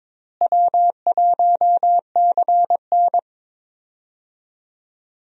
Morse code, heard as a keyed tone, about 22 wpm, 700 Hz. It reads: W1CN
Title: USA Call Signs - 4 Characters - Repeated 3x - 22wpm